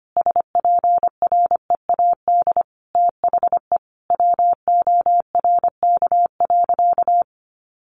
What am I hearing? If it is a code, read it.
Morse code, heard as a keyed tone, about 25 wpm, 700 Hz. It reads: SPREAD THE WORK.
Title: Sentences from Top 1400 Words 25wpm